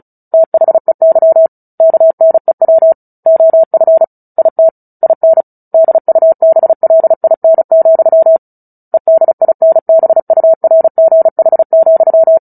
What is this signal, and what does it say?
Morse code, heard as a keyed tone, about 34 wpm, 650 Hz. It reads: THEY KNEW OF IT IN DUBLIN, EDINBURGH,
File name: Book Two / 09-Chapter 9 - 35 WPM